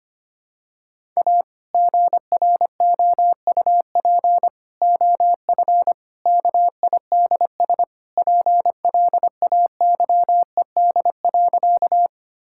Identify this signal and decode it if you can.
Morse code, heard as a keyed tone, about 25 wpm, 700 Hz. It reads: A GROUP OF KIDS PLAYED.
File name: practice / 25wpm / Sentences with 3-Word Phrases - 25wpm